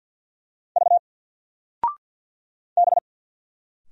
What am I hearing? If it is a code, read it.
Morse code, heard as a keyed tone, about 53 wpm, 700 Hz. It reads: V B
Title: Single Letter-Number Alt 50wpm